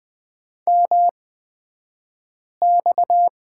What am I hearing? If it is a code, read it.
Morse code, heard as a keyed tone, about 20 wpm, spaced as for 5 wpm, 700 Hz. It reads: MX
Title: 228 Lesson 37d - Letter x - Two Characters 20wpm